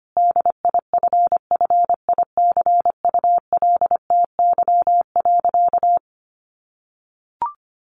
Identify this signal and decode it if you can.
Morse code, heard as a keyed tone, about 25 wpm, 700 Hz. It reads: DIFFICULTY.
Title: Sentences from Top 1500 Words 25wpm